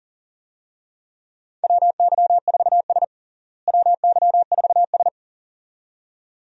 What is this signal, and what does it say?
Morse code, heard as a keyed tone, about 37 wpm, 700 Hz. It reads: WY4S WY4S
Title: USA Call Signs - 4 Characters - Repeated 2x - 40wpm